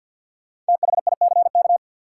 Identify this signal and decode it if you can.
Morse code, heard as a keyed tone, about 47 wpm, 700 Hz. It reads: THICK
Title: Sentences from Top 1600 Words - New Word Drill Alt - 50wpm